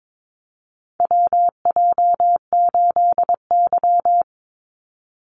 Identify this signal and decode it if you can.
Morse code, heard as a keyed tone, about 22 wpm, 700 Hz. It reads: WJ8Y
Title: Call Signs Repeated 2x - 22wpm